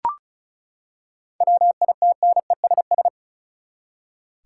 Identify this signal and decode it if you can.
Morse code, heard as a keyed tone, about 36 wpm, 700 Hz. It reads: WITNESS
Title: Seven Letter Words - 35wpm